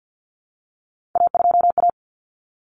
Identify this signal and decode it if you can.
Morse code, heard as a keyed tone, about 48 wpm, 700 Hz. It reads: A2A
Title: Common Cut Numbers - 3-Digit 50wpm